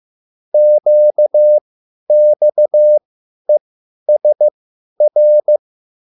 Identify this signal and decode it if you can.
Morse code, heard as a keyed tone, about 15 wpm, spaced as for 10 wpm, 600 Hz. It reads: QXESR